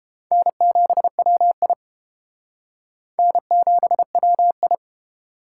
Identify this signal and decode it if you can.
Morse code, heard as a keyed tone, about 32 wpm, 700 Hz. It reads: N7WI N7WI